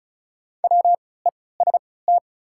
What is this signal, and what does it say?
Morse code, heard as a keyed tone, about 36 wpm, spaced as for 20 wpm, 690 Hz. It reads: WEST
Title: 3x Character Spacing - Top 500 Words - 35wpm